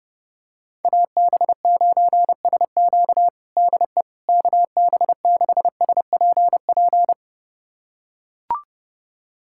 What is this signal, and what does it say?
Morse code, heard as a keyed tone, about 30 wpm, 700 Hz. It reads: AB9SQ DE KB6SPP
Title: USA Callsign DE Callsign 30wpm